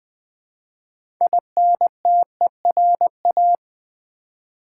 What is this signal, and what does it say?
Morse code, heard as a keyed tone, about 20 wpm, 700 Hz. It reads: INTERA
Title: Build A Word - 2 Letters At A Time - 8-Letter Words - 20wpm